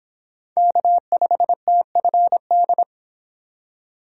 Morse code, heard as a keyed tone, about 26 wpm, 700 Hz. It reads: K5TFD
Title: Speed Racing - USA 1x3 Call Signs 22wpm